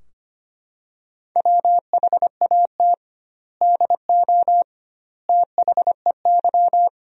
Morse code, heard as a keyed tone, about 25 wpm, 700 Hz. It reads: WHAT DO THEY
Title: 2x Word-Spacing - Sentences from Top 200 Words - 25wpm